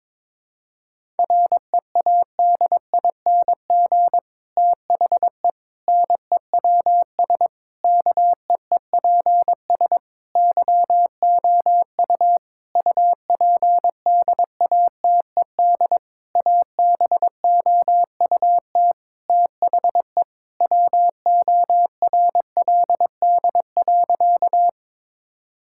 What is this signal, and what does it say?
Morse code, heard as a keyed tone, about 22 wpm, 700 Hz. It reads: READING THE NEWS KEEPS YOU UPDATED ABOUT THE WORLD.
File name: Difficult Sentences with Hints - 22wpm